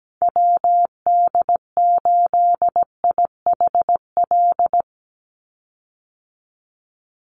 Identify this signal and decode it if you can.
Morse code, heard as a keyed tone, about 17 wpm, 700 Hz. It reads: WD8IHL
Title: USA Call Signs 17wpm